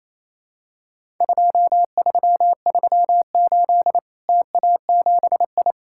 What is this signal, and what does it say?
Morse code, heard as a keyed tone, about 28 wpm, 700 Hz. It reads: 2338 TA7I
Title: Worked All Europe DX Contest - QTC Exchange - 28wpm